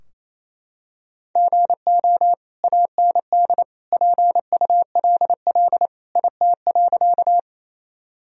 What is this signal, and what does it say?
Morse code, heard as a keyed tone, about 28 wpm, 700 Hz. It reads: GO AND PULL IT.
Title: Sentences from Top 1000 Words - Review Part 2 - 28wpm